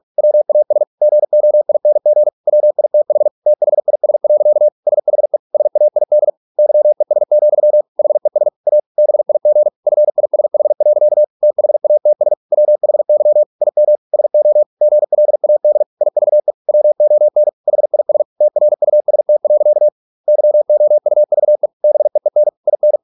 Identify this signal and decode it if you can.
Morse code, heard as a keyed tone, about 46 wpm, 600 Hz. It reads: WAS GOING WITH THIS. SHE SAID YES, HES A BIG FISH. THATS WHY IM SO GLAD IVE WON HIS TRUST. YOUVE BEEN IN